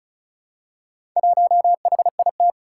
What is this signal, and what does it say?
Morse code, heard as a keyed tone, about 36 wpm, 700 Hz. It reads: 1HIT
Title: Four Challenging Letter-Number Combinations 35wpm